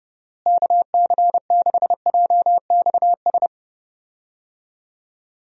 Morse code, heard as a keyed tone, about 32 wpm, 700 Hz. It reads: KC6JXS
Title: USA Call Signs - 6 Characters - Repeated 2x - 30wpm